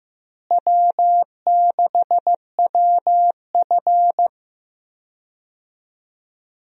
Morse code, heard as a keyed tone, about 15 wpm, 700 Hz. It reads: W6WF
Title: USA Call Signs - 4 Characters - Repeated 2x - 15wpm